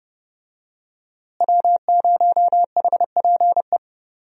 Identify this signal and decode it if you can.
Morse code, heard as a keyed tone, about 32 wpm, 700 Hz. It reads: W0HPE
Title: USA Call Signs - 5 Characters - Repeated 3x - 30wpm